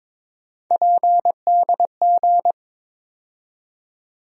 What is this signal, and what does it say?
Morse code, heard as a keyed tone, about 22 wpm, 700 Hz. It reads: PDG
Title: USA Call Sign Postfixes - 3 Letters 22wpm